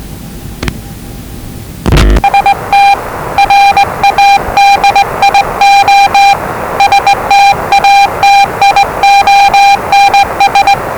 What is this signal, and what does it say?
Morse code, heard as a keyed tone, about 20 wpm, 800 Hz, with noise. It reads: ST RADIO STATIONS
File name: WellingtonRadioZlwCloseDownOnHfAndMf / 01-Zlw 4277khz 417.5 qrt until 0925z